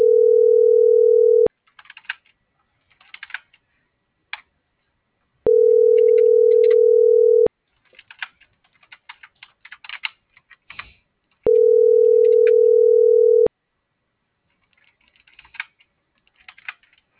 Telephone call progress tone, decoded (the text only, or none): ringback tone